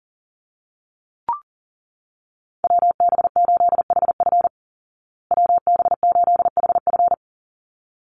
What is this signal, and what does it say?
Morse code, heard as a keyed tone, about 37 wpm, 700 Hz. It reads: WB8HF WB8HF
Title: Call Signs Repeated 3x - 40wpm